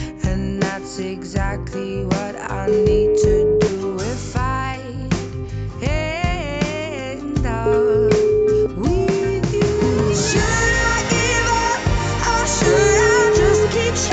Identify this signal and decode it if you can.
Ringback tone (Europe, 425 Hz), with noise.